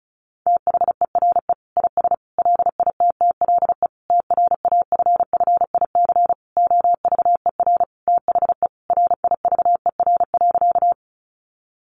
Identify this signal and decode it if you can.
Morse code, heard as a keyed tone, about 34 wpm, 700 Hz. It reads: THERE IS LITTLE TRAFFIC OVER THE RIVER.
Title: Sentences from Top 1300 Words 35wpm